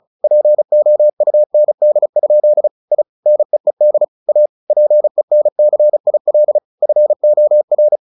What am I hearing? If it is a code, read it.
Morse code, heard as a keyed tone, about 34 wpm, 600 Hz. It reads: POUND? I NEED A PENCIL FOR
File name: the old man and the sea 2 35wpm